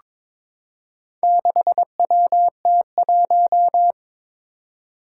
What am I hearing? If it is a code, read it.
Morse code, heard as a keyed tone, about 22 wpm, 700 Hz. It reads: 6WT1
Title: Four Challenging Letter-Number Combinations 22wpm